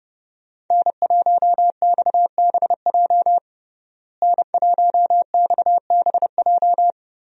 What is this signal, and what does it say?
Morse code, heard as a keyed tone, about 30 wpm, 700 Hz. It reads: N1XBJ N1XBJ
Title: USA Call Signs - 5 Characters - Repeated 2x - 30wpm